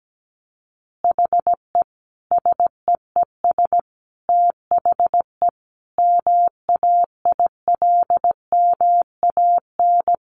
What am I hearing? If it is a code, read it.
Morse code, heard as a keyed tone, about 17 wpm, 700 Hz. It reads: HE SEES THE MAILMAN
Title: Additional Easy Phrases 17wpm